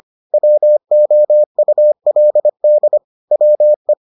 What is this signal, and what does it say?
Morse code, heard as a keyed tone, about 25 wpm, 600 Hz. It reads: WOULD WE